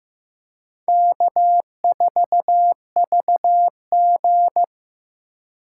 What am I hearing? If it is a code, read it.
Morse code, heard as a keyed tone, about 18 wpm, 700 Hz. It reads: K4VG
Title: Speed Racing - USA Call Signs 15wpm